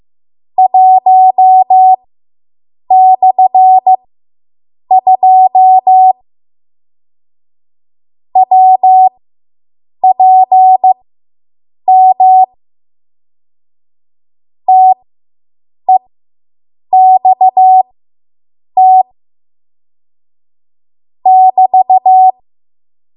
Morse code, heard as a keyed tone, about 15 wpm, spaced as for 7 wpm, 750 Hz. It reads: 1/2 WPM TEXT <BT>